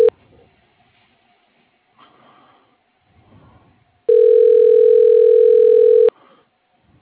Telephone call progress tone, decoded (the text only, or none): ringback tone